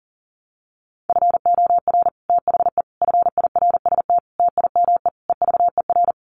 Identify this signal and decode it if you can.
Morse code, heard as a keyed tone, about 40 wpm, 700 Hz. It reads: FOR THE FIRST TIME EVER
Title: Sets of 5 Words Encore2 40wpm